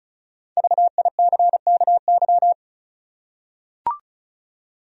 Morse code, heard as a keyed tone, about 36 wpm, 700 Hz. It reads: VICKY
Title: USA Common Names - Max 5 Letters - 35wpm